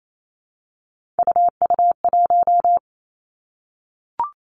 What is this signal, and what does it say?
Morse code, heard as a keyed tone, about 28 wpm, 700 Hz. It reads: UU1
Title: Three Challenging Letter-Number Combinations 28wpm